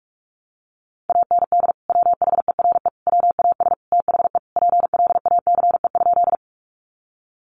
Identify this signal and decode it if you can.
Morse code, heard as a keyed tone, about 44 wpm, 700 Hz. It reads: AND WHERE WAS THE PLACE?
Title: Sentences from Top 200 Words 45wpm